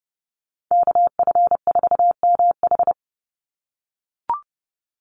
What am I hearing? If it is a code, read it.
Morse code, heard as a keyed tone, about 30 wpm, 700 Hz. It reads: KF4MH
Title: NA SOTA Chaser Callsigns 30wpm